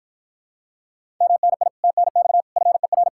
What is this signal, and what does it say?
Morse code, heard as a keyed tone, about 52 wpm, 700 Hz. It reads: MNI TNX FER